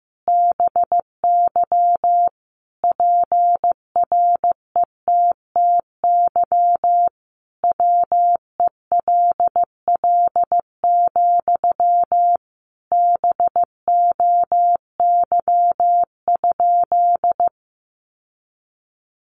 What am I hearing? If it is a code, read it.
Morse code, heard as a keyed tone, about 15 wpm, 700 Hz. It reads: BY PRETTY WELL, BOY?
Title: Sentences from Top 1000 Words 15wpm